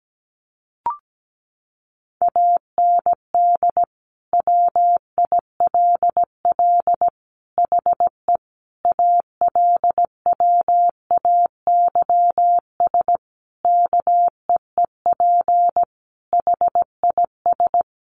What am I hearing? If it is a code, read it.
Morse code, heard as a keyed tone, about 17 wpm, 700 Hz. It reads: AND WILL HE ALWAYS KEEP HIS
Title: Sentences from Top 1200 Words 17wpm